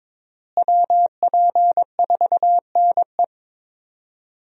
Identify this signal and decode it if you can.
Morse code, heard as a keyed tone, about 22 wpm, 700 Hz. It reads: WP4NE
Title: USA Call Signs - 5 Characters - Repeated 2x - 22wpm